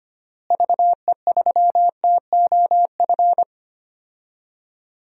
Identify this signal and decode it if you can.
Morse code, heard as a keyed tone, about 25 wpm, 700 Hz. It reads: VE3TOF
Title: DXCC-100 Call Signs - 6 Characters - Repeated 3x - 25wpm